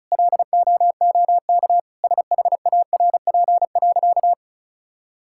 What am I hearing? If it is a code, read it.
Morse code, heard as a keyed tone, about 36 wpm, 700 Hz. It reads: LOOK SHARP.
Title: Sentences from Top 1600 Words 35wpm